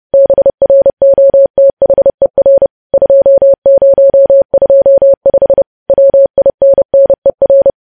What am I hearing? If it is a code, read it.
Morse code, heard as a keyed tone, about 30 wpm, 550 Hz. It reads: BROTHER 2025 WINNER
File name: podcast / News headlines-1x-30x30